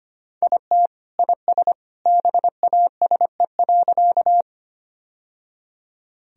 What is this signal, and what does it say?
Morse code, heard as a keyed tone, about 25 wpm, 700 Hz. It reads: IT IS BASE.
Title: Sentences from Top 1000 Words - Review Part 1 - 25wpm